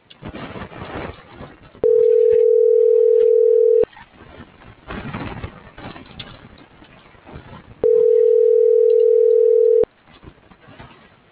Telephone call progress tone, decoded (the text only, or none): ringback tone